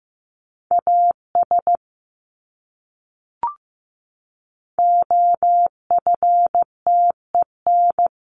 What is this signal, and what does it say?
Morse code, heard as a keyed tone, about 14 wpm, 700 Hz. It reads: AS OFTEN
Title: Top 500 Words Repeated 15wpm